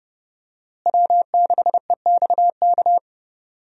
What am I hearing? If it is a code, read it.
Morse code, heard as a keyed tone, about 30 wpm, 700 Hz. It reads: W6EXK
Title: USA Call Signs - 5 Characters - Repeated 2x - 30wpm